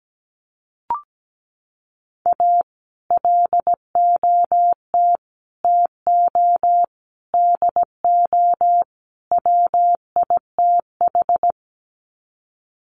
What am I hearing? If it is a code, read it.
Morse code, heard as a keyed tone, about 17 wpm, 700 Hz. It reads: A LOT TO DO WITH
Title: Sets of 5 Words 17wpm